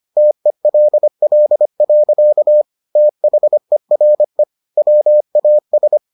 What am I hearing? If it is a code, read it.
Morse code, heard as a keyed tone, about 25 wpm, 600 Hz. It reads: TELL. THERE WAS